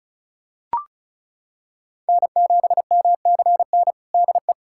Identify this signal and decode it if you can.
Morse code, heard as a keyed tone, about 36 wpm, 700 Hz. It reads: N7MCN DE
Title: USA Callsign DE Callsign - 5 Characters - 35wpm